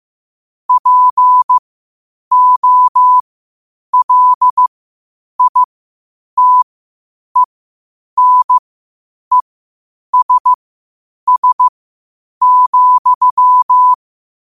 Morse code, heard as a keyed tone, about 15 wpm, spaced as for 8 wpm, 1000 Hz. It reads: POLITENESS,
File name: qotdfiles / 8wpm